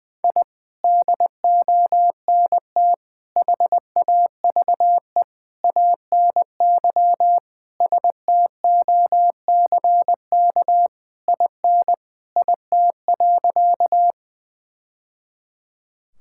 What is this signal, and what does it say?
Morse code, heard as a keyed tone, about 21 wpm, 700 Hz. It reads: I DONT HAVE ANY STOCK IN IT.
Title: Sentences from Top 1000 Words - Review Part 5 - 20wpm